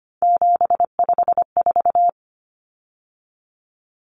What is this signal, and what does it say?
Morse code, heard as a keyed tone, about 25 wpm, 700 Hz. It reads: 754
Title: Common Cut Numbers - 3-Digit 25wpm